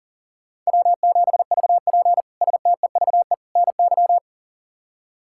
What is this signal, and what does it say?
Morse code, heard as a keyed tone, about 40 wpm, 700 Hz. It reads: W7VP STEVE NY
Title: K1USN SST QSO Practice - 40wpm